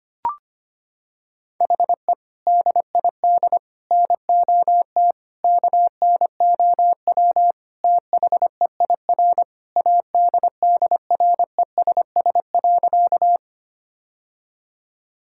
Morse code, heard as a keyed tone, about 25 wpm, 700 Hz. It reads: HE DID NOT KNOW THEIR ADDRESS.
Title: Sentences from Top 1000 Words - Review Part 6 - 25wpm